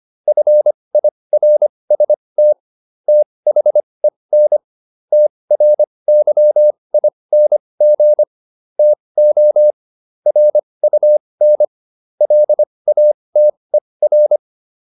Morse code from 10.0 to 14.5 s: RUN LATER

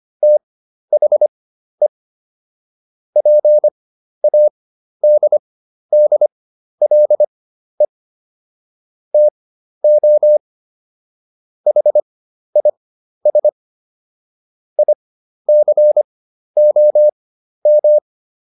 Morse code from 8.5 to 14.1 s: TO HIS

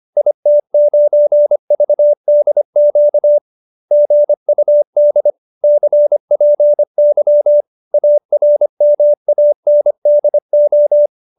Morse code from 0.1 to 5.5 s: IT9VDQ GUD

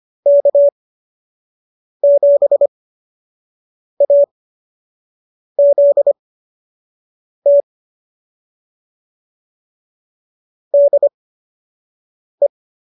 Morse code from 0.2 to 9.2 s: K7AZT

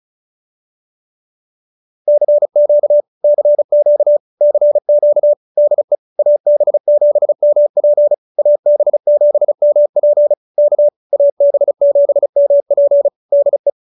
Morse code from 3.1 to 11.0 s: CQ CQ DE AB7MP AB7MP K

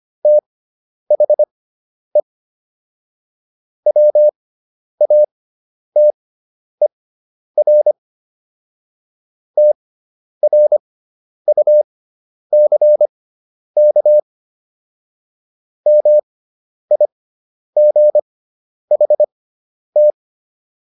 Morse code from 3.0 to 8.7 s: WATER

